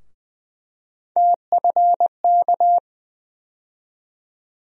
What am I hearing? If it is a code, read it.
Morse code, heard as a keyed tone, about 20 wpm, 700 Hz. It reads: TFK
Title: USA Call Sign Postfixes - 3 Letters 20wpm